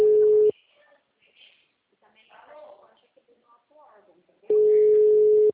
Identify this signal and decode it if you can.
Ringback tone (Europe, 425 Hz), with speech.